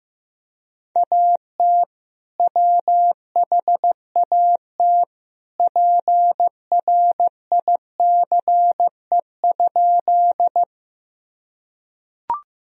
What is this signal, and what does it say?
Morse code, heard as a keyed tone, about 15 wpm, 700 Hz. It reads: AT WHAT PRICE?
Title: Sentences from Top 1000 Words - Review Part 4 - 15wpm